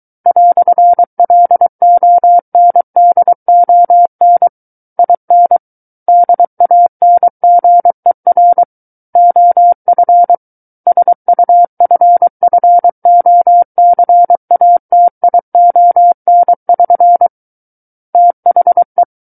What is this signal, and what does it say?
Morse code, heard as a keyed tone, about 23 wpm, 700 Hz. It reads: "LONDON IN DANGER OF SUFFOCATION<SN> THE
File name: Book One / 14-Chapter 14 - 23 WPM